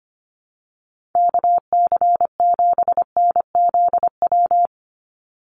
Morse code, heard as a keyed tone, about 25 wpm, 700 Hz. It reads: KC7NZW
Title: USA Call Signs 25wpm